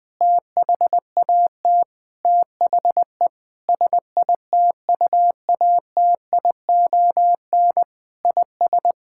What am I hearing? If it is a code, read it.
Morse code, heard as a keyed tone, about 20 wpm, 700 Hz. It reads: THAT THE SITUATION IS
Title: Sentences with 8-Word Phrases - 20wpm